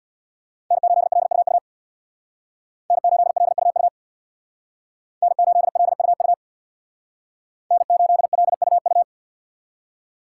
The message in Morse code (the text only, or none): N8LUU N8LUU N8LUU N8LUU